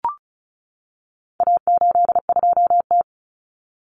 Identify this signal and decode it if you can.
Morse code, heard as a keyed tone, about 34 wpm, 700 Hz. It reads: A82T
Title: Common Cut Numbers - 4-Digit 35wpm